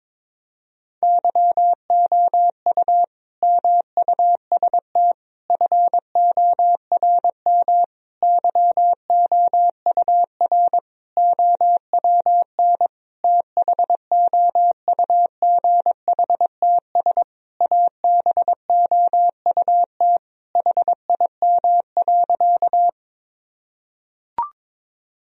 Morse code, heard as a keyed tone, about 22 wpm, 700 Hz. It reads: YOU MUST FORM YOUR OWN THOUGHTS ABOUT HIM.
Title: Sentences from Top 1000 Words - Review Part 1 - 22wpm